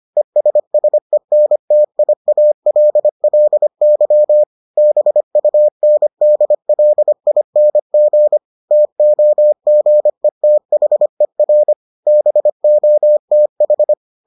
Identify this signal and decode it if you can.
Morse code, heard as a keyed tone, about 25 wpm, 600 Hz. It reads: ESSENTIALLY BUNDLING TOGETHER BOTH